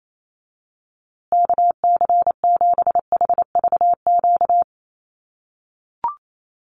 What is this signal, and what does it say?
Morse code, heard as a keyed tone, about 28 wpm, 700 Hz. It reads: KC7HVQ
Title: USA Call Signs Repeated 3x 28wpm